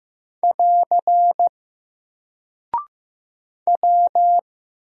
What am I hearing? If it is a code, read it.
Morse code, heard as a keyed tone, about 15 wpm, 700 Hz. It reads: <AR> W
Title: Single Character (Letter:Number:Punctuation:Prosign) 15wpm